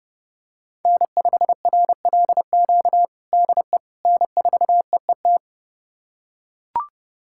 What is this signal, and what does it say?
Morse code, heard as a keyed tone, about 29 wpm, 700 Hz. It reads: N5RLQ DE N4EET
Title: USA Callsign DE Callsign - 5 Characters - 30wpm